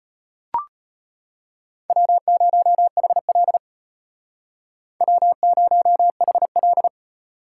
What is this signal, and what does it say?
Morse code, heard as a keyed tone, about 32 wpm, 700 Hz. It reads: W0HL W0HL